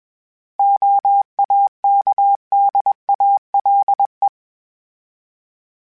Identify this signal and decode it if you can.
Morse code, heard as a keyed tone, about 21 wpm, 800 Hz. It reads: OAKDALE
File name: fromSMs / FromSMs mp3s / Cities@20